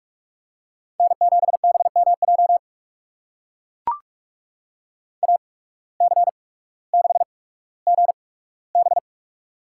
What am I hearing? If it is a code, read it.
Morse code, heard as a keyed tone, about 44 wpm, spaced as for 19 wpm, 700 Hz. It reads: N7BMJ AC6GB
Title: USA Call Signs - 5 Characters - 8x Character Spacing - 45wpm